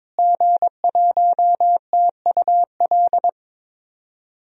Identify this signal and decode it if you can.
Morse code, heard as a keyed tone, about 22 wpm, 700 Hz. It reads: G1TUL
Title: DXCC-100 Call Signs - 5 Characters - Repeated 3x - 22wpm